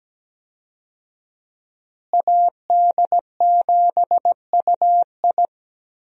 Morse code, heard as a keyed tone, about 17 wpm, 700 Hz. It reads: AD7UI
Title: USA Call Signs - 5 Characters - Repeated 3x - 17wpm